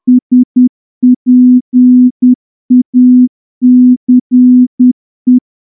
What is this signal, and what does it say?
Morse code, heard as a keyed tone, about 10 wpm, 250 Hz. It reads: SPACE
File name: morse-training-10WPM